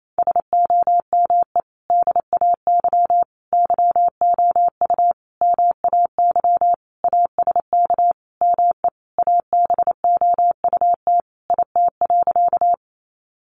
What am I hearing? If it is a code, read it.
Morse code, heard as a keyed tone, about 28 wpm, 700 Hz. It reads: SOME DAY YOU MAY ASK ME ABOUT IT.
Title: Sentences from Top 200 Words 28wpm